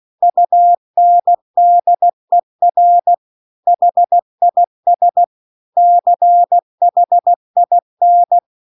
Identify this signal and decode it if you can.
Morse code, heard as a keyed tone, about 16 wpm, 700 Hz. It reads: UNDER HIS CHIN
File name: CW-The-Big-Twoheaded-River-2-16WPM